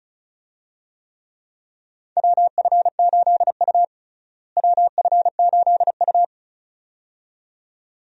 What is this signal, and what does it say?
Morse code, heard as a keyed tone, about 36 wpm, 700 Hz. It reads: WF8U WF8U
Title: USA Call Signs - 4 Characters - Repeated 2x - 35wpm